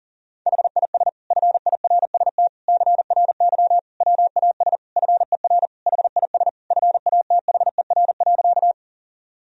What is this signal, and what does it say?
Morse code, heard as a keyed tone, about 40 wpm, 700 Hz. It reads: HIS FIRST CRY WAS FER HIS FATHER.
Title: Sentences from Top 500 Words and Abbreviations 40wpm